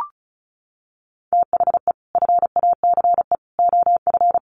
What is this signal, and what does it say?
Morse code, heard as a keyed tone, about 34 wpm, 700 Hz. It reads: THE FACE OF
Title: Sets of 3 Words 35wpm